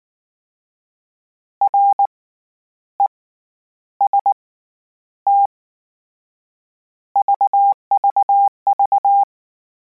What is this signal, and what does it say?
Morse code, heard as a keyed tone, about 19 wpm, 800 Hz. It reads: R E S T VVV
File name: Combinations / Combinations-mp3s / Combinations@18